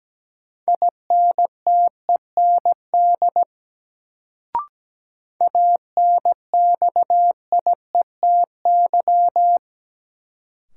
Morse code, heard as a keyed tone, about 17 wpm, 700 Hz. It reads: INTEND ANXIETY
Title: Sentences from Top 1600 Words - New Word Drill Alt - 17wpm